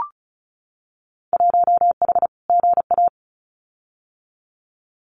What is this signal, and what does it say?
Morse code, heard as a keyed tone, about 36 wpm, 700 Hz. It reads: 1H GA
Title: Winter Field Day Exchanges 35wpm